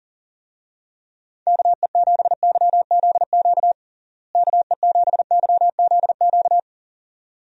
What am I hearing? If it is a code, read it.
Morse code, heard as a keyed tone, about 40 wpm, 700 Hz. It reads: KE7YZQ KE7YZQ
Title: USA Call Signs - 6 Characters - Repeated 2x - 40wpm